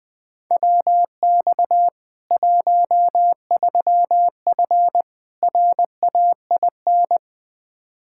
Morse code, heard as a keyed tone, about 19 wpm, 700 Hz. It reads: WX 13F RAIN
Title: K1USN SST QSO Practice 20wpm